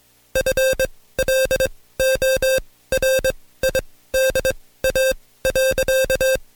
Morse code, heard as a keyed tone, about 22 wpm, spaced as for 16 wpm, 530 Hz. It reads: FLORIDA.
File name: Code Quick / AudioDownld2 / Seg134-Stepper7 15WPM